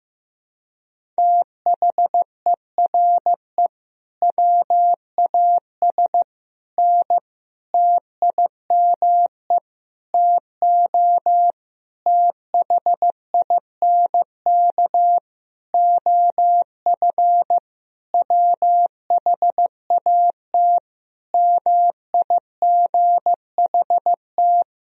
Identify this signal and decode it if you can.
Morse code, heard as a keyed tone, about 15 wpm, 700 Hz. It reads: THERE WAS N TIME TO THINK OF WHAT MIGHT
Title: Sentences from Top 400 Words and Abbreviations 15wpm